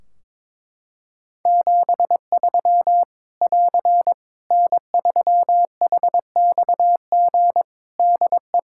Morse code, heard as a keyed tone, about 22 wpm, 700 Hz. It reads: 73 <AR> N3HXG DE
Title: Practice QSO - Exchange 4 - Ending - 22wpm